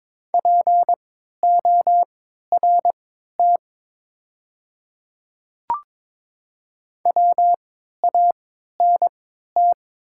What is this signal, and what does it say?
Morse code, heard as a keyed tone, about 22 wpm, spaced as for 13 wpm, 700 Hz. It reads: PORT WANT
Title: 3x Character Spacing - Top 300 Words - 22wpm